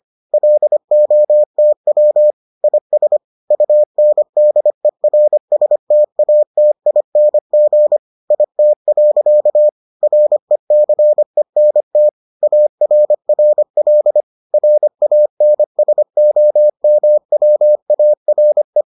Morse code, heard as a keyed tone, about 25 wpm, 600 Hz. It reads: LOTW IS UNDERSTATING IT. RECENT ARRL RANSOMWARE